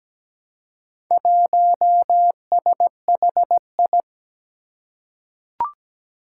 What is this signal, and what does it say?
Morse code, heard as a keyed tone, about 17 wpm, 700 Hz. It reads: 1SHI
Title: Four Challenging Letter-Number Combinations 17wpm